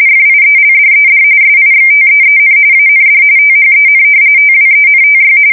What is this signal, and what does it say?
RTTY, about 45.45 baud. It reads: FIELDED SIXTEEN AND THEN TWENTY
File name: fsk170 / battleshipnj